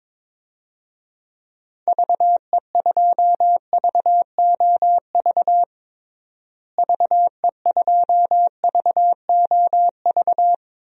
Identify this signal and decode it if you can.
Morse code, heard as a keyed tone, about 22 wpm, 700 Hz. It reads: VE2VOV VE2VOV
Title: DXCC-100 Call Signs - 6 Characters - Repeated 3x - 22wpm